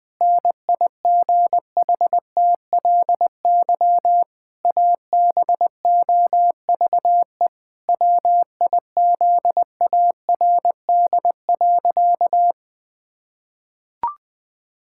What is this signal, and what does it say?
Morse code, heard as a keyed tone, about 20 wpm, 700 Hz. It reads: NIGHTLY ABOVE WIZARD.
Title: Pangram Challenge v2 - 20wpm